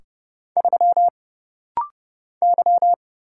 Morse code, heard as a keyed tone, about 30 wpm, 700 Hz. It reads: 3 Y
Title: Single Letter-Number Alt - Rapid-Fire 30wpm